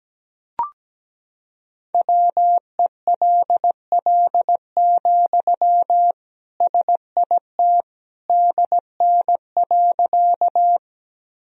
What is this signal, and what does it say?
Morse code, heard as a keyed tone, about 17 wpm, 700 Hz. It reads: WELL, SIT DN.
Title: Sentences from Top 500 Words and Abbreviations 17wpm